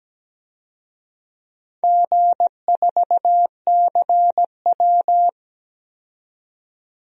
Morse code, heard as a keyed tone, about 17 wpm, 700 Hz. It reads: G4CW
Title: DXCC-100 Call Signs - 4 Characters - 17wpm